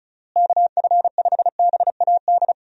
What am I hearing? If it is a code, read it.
Morse code, heard as a keyed tone, about 36 wpm, 700 Hz. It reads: KF5BAD
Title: USA Call Signs - 6 Characters - Repeated 2x - 35wpm